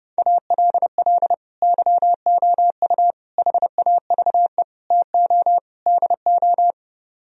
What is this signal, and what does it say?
Morse code, heard as a keyed tone, about 36 wpm, 700 Hz. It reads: ALL YOU HAVE TO DO
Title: Speed-Racing - Sets of 5 Words 30wpm